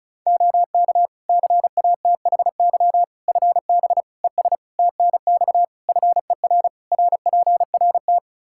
Morse code, heard as a keyed tone, about 36 wpm, 700 Hz. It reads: OK CATHY FB ES TNX FER RPRT
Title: Practice QSO - Exchange 2 - Info - 35wpm